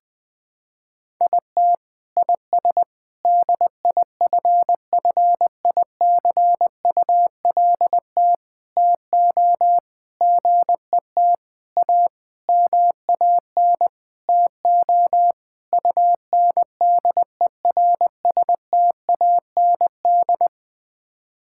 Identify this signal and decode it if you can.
Morse code, heard as a keyed tone, about 20 wpm, 700 Hz. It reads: IT IS DIFFICULT TO GET A MAN TO UNDERSTAND
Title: Sets of 9 Words - 20wpm